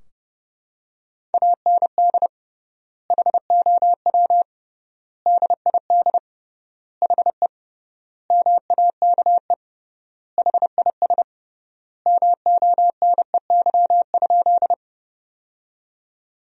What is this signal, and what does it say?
Morse code, heard as a keyed tone, about 32 wpm, 700 Hz. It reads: AND HOW DID HE MAKE HIS MONEY?
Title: 3x Word-Spacing - Sentences from Top 400 Words - 30wpm